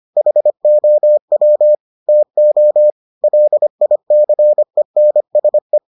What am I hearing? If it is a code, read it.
Morse code, heard as a keyed tone, about 25 wpm, 600 Hz. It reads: HOW TO LICENSE